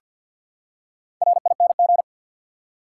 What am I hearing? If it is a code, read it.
Morse code, heard as a keyed tone, about 51 wpm, 700 Hz. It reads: RING
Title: Four Letter Words - 50wpm